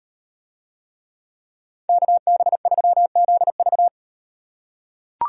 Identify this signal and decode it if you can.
Morse code, heard as a keyed tone, about 37 wpm, 700 Hz. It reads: KB3ZV